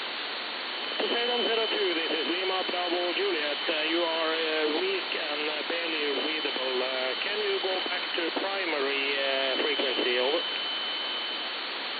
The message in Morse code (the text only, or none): none